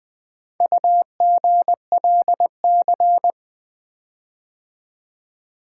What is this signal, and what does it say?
Morse code, heard as a keyed tone, about 24 wpm, 700 Hz. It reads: UGLC
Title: Speed-Racing - Four Letter-Number Combinations - 20wpm